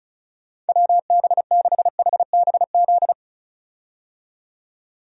Morse code, heard as a keyed tone, about 36 wpm, 700 Hz. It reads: WB6HBZ
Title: USA Call Signs - 6 Characters - 35wpm